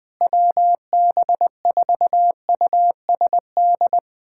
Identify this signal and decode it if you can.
Morse code, heard as a keyed tone, about 20 wpm, 700 Hz. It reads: WB4USD
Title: USA Callsign DE Callsign - 6 Characters - 20wpm